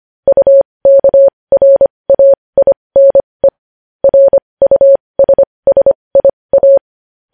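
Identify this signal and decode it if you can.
Morse code, heard as a keyed tone, about 25 wpm, spaced as for 20 wpm, 550 Hz. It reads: UKRAINE RUSSIA
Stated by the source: Morse code news headlines sent at 20.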